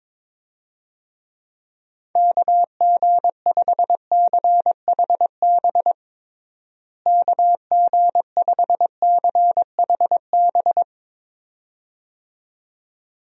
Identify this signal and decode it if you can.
Morse code, heard as a keyed tone, about 22 wpm, 700 Hz. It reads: KG5CHB KG5CHB
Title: USA Call Signs - 6 Characters - Repeated 2x - 22wpm